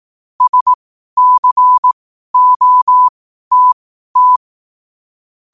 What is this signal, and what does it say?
Morse code, heard as a keyed tone, about 18 wpm, spaced as for 12 wpm, 1000 Hz. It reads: SCOTT
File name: qotdfiles / 12wpm